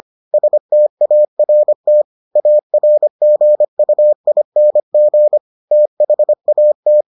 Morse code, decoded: START ARGUING THAT

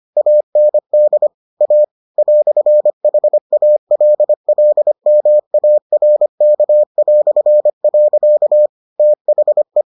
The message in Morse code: AND A "HALLMARK". THE

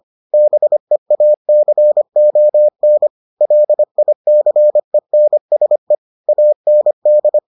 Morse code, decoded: BEACON LICENSE AND

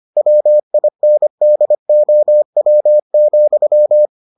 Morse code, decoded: WINDOW,

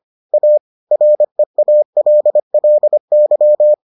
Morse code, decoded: A REALLY

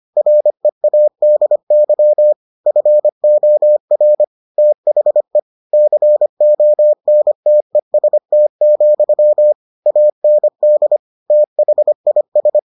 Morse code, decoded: READY FOR THE CONTEST, AND THIS

